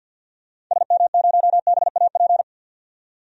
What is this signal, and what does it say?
Morse code, heard as a keyed tone, about 49 wpm, 700 Hz. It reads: SM0BAP
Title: DXCC-100 Call Signs - 6 Characters - 50wpm